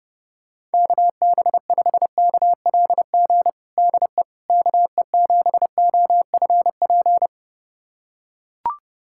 Morse code, heard as a keyed tone, about 30 wpm, 700 Hz. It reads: KB5KLG DE KE7OFP
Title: USA Callsign DE Callsign 30wpm